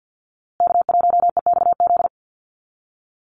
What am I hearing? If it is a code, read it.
Morse code, heard as a keyed tone, about 50 wpm, 700 Hz. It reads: K1EXZ
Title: USA Call Signs Repeated 2x 50wpm